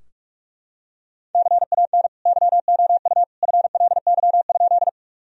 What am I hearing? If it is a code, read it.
Morse code, heard as a keyed tone, about 44 wpm, 700 Hz. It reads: CAN YOU FLY?
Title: Sentences from Top 1000 Words - Review Part 2 - 45wpm